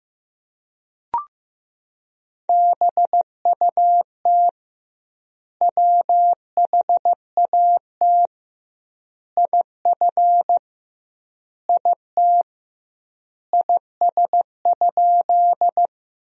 Morse code, heard as a keyed tone, about 16 wpm, 700 Hz. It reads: BUT WHAT IF IT IS?
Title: 2x Word-Spacing - Sentences from Top 100 Words - 15wpm